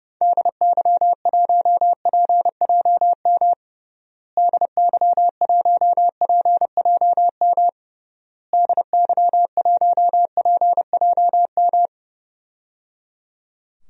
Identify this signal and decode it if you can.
Morse code, heard as a keyed tone, about 29 wpm, 700 Hz. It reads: DY1PJM DY1PJM DY1PJM
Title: DXCC-100 Call Signs - 6 Characters - Repeated 3x - 30wpm